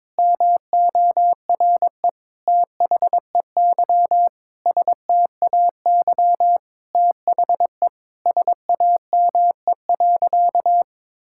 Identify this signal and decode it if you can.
Morse code, heard as a keyed tone, about 21 wpm, 700 Hz. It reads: MORE THEY STAY THE SAME.
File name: practice / 22wpm / Sentences with 8-Word Phrases - 22wpm